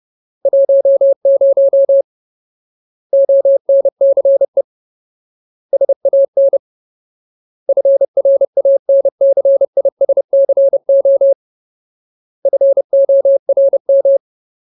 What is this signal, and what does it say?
Morse code, heard as a keyed tone, about 30 wpm, 550 Hz. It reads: 10 ONCE SAN FRANCISCO FORM
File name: Ultimate Word List / 30WPM Ultimate